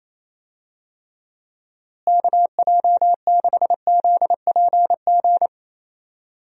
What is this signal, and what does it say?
Morse code, heard as a keyed tone, about 28 wpm, 700 Hz. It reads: KJ6ZPG
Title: USA Call Signs - 6 Characters - Repeated 2x - 28wpm